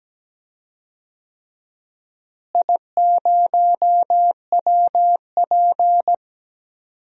Morse code, heard as a keyed tone, about 17 wpm, 700 Hz. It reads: I0WP
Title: DXCC-100 Call Signs - 4 Characters - Repeated 3x - 17wpm